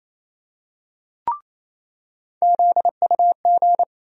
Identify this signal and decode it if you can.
Morse code, heard as a keyed tone, about 28 wpm, 700 Hz. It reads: ZUG
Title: USA Call Sign Postfixes - 3 Letters 28wpm